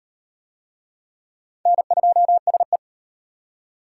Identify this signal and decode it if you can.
Morse code, heard as a keyed tone, about 37 wpm, 700 Hz. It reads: N2SE